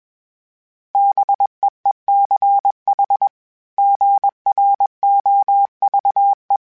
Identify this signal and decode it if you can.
Morse code, heard as a keyed tone, about 20 wpm, 800 Hz. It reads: BEECH GROVE
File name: fromSMs / FromSMs mp3s / Cities@20